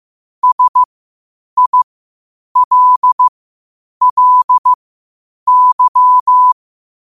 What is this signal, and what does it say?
Morse code, heard as a keyed tone, about 15 wpm, spaced as for 8 wpm, 1000 Hz. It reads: SILLY
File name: qotdfiles / 8wpm